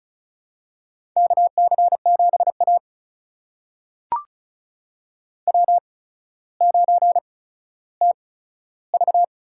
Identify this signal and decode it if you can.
Morse code, heard as a keyed tone, about 36 wpm, spaced as for 15 wpm, 700 Hz. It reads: KC7A W9TV
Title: USA Call Signs - 4 Characters - 8x Character Spacing - 35wpm